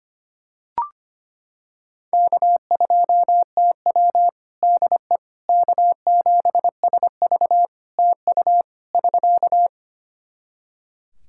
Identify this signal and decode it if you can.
Morse code, heard as a keyed tone, about 25 wpm, 700 Hz. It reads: K2TW DE K7SV TU <SK>
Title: K1USN SST QSO Practice 25wpm